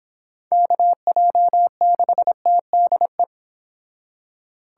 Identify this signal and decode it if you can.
Morse code, heard as a keyed tone, about 29 wpm, 700 Hz. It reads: KJ6TDE